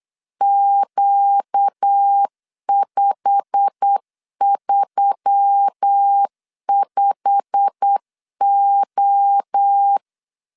Morse code, heard as a keyed tone, about 8 wpm, 800 Hz. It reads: Q535O
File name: LEZIONI / 8 YJ45 / 8 YJ45 8wpm 3214